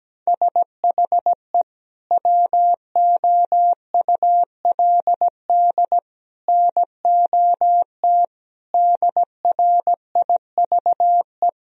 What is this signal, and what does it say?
Morse code, heard as a keyed tone, about 17 wpm, 700 Hz. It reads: SHE WOULD NOT DRIVE
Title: Sentences from Top 1000 Words - Review Part 3 - 17wpm